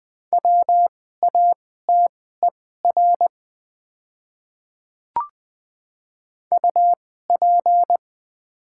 Morse code, heard as a keyed tone, about 20 wpm, spaced as for 14 wpm, 700 Hz. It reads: WATER UP
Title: 2x Character Spacing - Top 100 Words - 20wpm